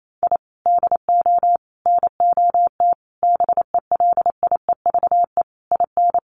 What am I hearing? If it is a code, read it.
Morse code, heard as a keyed tone, about 28 wpm, 700 Hz. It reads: I DO NOT BELIEVE IN
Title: Sets of 5 Words Encore 28wpm